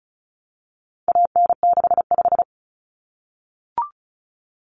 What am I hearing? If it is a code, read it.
Morse code, heard as a keyed tone, about 34 wpm, 700 Hz. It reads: AN65
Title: Common Cut Numbers - 4-Digit 35wpm